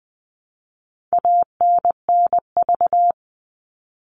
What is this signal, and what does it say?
Morse code, heard as a keyed tone, about 20 wpm, 700 Hz. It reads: ANNV
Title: All Cut Numbers - 4-Digit 20wpm